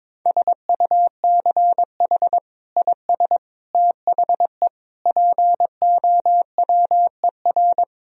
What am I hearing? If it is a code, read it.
Morse code, heard as a keyed tone, about 22 wpm, 700 Hz. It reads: SUCH IS THE POWER
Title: Sentences from Top 1000 Words - Review Part 2 - 22wpm